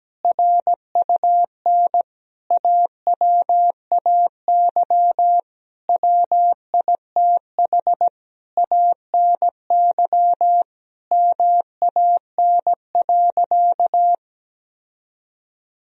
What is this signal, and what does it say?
Morse code, heard as a keyed tone, about 17 wpm, 700 Hz. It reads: RUN AWAY WITH ANY MAN.
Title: Sentences from Top 1000 Words - Review Part 1 - 17wpm